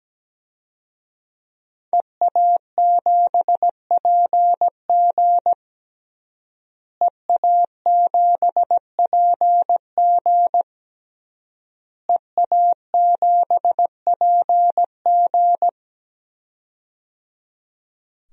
Morse code, heard as a keyed tone, about 17 wpm, 700 Hz. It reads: EA7PG EA7PG EA7PG
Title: DXCC-100 Call Signs - 5 Characters - Repeated 3x - 17wpm